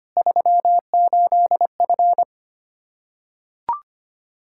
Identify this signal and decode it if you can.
Morse code, heard as a keyed tone, about 25 wpm, 700 Hz. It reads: 38F
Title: Practice QSO - Exchange 2 - Info - 25wpm